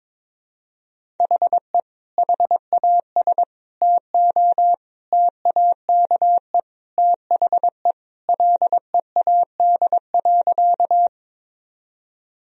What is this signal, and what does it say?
Morse code, heard as a keyed tone, about 22 wpm, 700 Hz. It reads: HE HAS TO TAKE THE LEAD.
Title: Sentences from Top 1000 Words - Review Part 2 - 22wpm